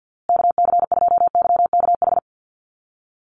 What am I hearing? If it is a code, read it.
Morse code, heard as a keyed tone, about 50 wpm, 700 Hz. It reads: KC2YKH
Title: USA Call Signs Repeated 2x 50wpm